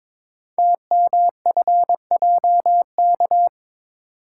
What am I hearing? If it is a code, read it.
Morse code, heard as a keyed tone, about 22 wpm, 700 Hz. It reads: TMFJK
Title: Code Groups - 5 Letter:Numbers - 22wpm